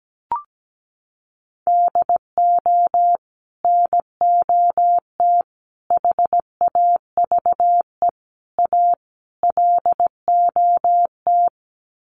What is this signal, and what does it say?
Morse code, heard as a keyed tone, about 17 wpm, 700 Hz. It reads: DO NOT HAVE A LOT
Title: Sets of 5 Words Encore 17wpm